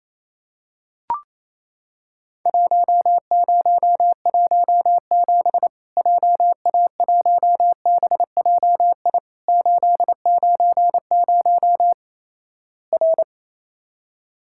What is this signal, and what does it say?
Morse code, heard as a keyed tone, about 28 wpm, 700 Hz. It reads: 1017 JA1BJI 890 R
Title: Worked All Europe DX Contest - QTC Exchange - 28wpm